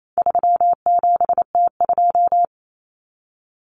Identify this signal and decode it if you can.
Morse code, heard as a keyed tone, about 28 wpm, 700 Hz. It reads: 37T2
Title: Common Cut Numbers - 4-Digit 28wpm